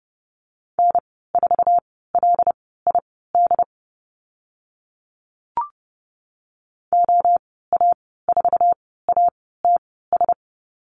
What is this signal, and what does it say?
Morse code, heard as a keyed tone, about 30 wpm, spaced as for 17 wpm, 700 Hz. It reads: N4LID OA4ATS
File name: Lesson 15g - Number 5 - Call Signs 30wpm